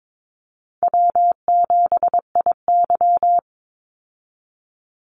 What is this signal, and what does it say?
Morse code, heard as a keyed tone, about 22 wpm, 690 Hz. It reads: W7IY
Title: Easy Call Signs Repeated 2x - 22wpm